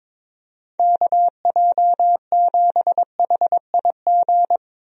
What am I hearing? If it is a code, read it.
Morse code, heard as a keyed tone, about 22 wpm, 700 Hz. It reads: KJ7HIG
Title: USA Call Signs - 6 Characters - 22wpm